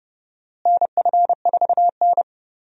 Morse code, heard as a keyed tone, about 30 wpm, 700 Hz. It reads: NF4N
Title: USA Call Signs - 4 Characters - Repeated 2x - 30wpm